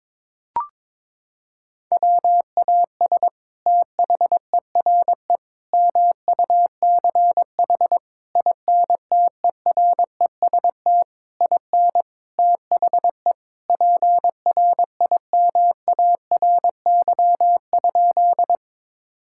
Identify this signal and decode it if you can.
Morse code, heard as a keyed tone, about 22 wpm, 700 Hz. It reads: WAS THERE MUCH INTEREST IN THE PRIMARY?
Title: Sentences from Top 1000 Words - Review Part 5 - 22wpm